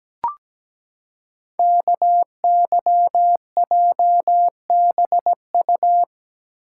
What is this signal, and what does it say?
Morse code, heard as a keyed tone, about 17 wpm, 700 Hz. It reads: KYJBU
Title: Code Groups - 5 Letters - 17wpm